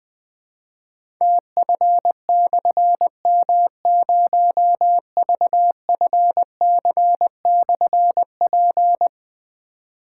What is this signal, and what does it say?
Morse code, heard as a keyed tone, about 20 wpm, 700 Hz. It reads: TF/M0VFC/P
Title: Difficult Call Signs 20wpm